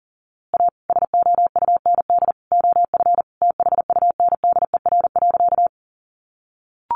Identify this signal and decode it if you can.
Morse code, heard as a keyed tone, about 40 wpm, 700 Hz. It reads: A SOUND OF THUNDER.
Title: A Sound Of Thunder 40wpm